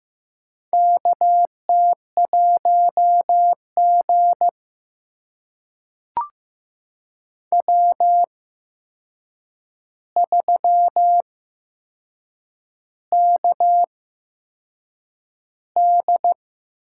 Morse code, heard as a keyed tone, about 15 wpm, spaced as for 6 wpm, 700 Hz. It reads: KT1G W3KD
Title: USA Call Signs - 4 Characters - 8x Character Spacing - 15wpm